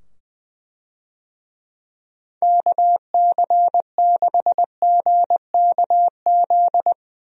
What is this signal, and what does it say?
Morse code, heard as a keyed tone, about 20 wpm, 700 Hz. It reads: KC6GKZ
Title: USA Call Signs - 6 Characters - 20wpm